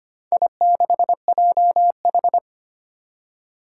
Morse code, heard as a keyed tone, about 25 wpm, 700 Hz. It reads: I6JH
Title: Four Challenging Letter-Number Combinations 25wpm